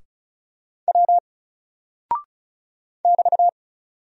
Morse code, heard as a keyed tone, about 34 wpm, 700 Hz. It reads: W <BT>
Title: Single Character (Letter:Number:Punctuation:Prosign) Alt 35wpm